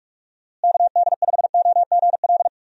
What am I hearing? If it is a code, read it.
Morse code, heard as a keyed tone, about 47 wpm, 700 Hz. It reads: KD5OGL
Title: USA Call Signs - 6 Characters - Repeated 3x - 45wpm